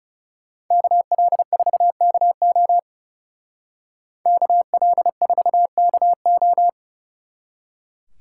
Morse code, heard as a keyed tone, about 35 wpm, 700 Hz. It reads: KL4KO KL4KO
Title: Speed Racing - USA 2x2 Call Signs 30wpm